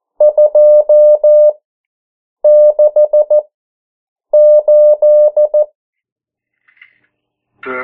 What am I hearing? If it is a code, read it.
Morse code, heard as a keyed tone, about 14 wpm, spaced as for 7 wpm, 600 Hz. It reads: 268
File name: morse mp3 / 09 urok 8